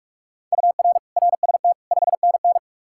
Morse code, heard as a keyed tone, about 44 wpm, 700 Hz. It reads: UR RST 5NN